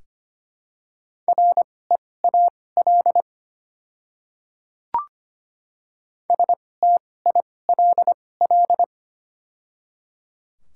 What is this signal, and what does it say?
Morse code, heard as a keyed tone, about 25 wpm, spaced as for 18 wpm, 700 Hz. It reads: REAL STILL
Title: 2x Character Spacing - Top 300 Words - 25wpm